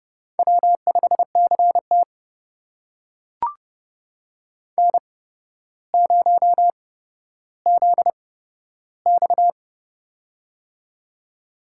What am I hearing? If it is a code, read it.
Morse code, heard as a keyed tone, about 30 wpm, spaced as for 13 wpm, 700 Hz. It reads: W5CT N0ZX
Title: USA Call Signs - 4 Characters - 8x Character Spacing - 30wpm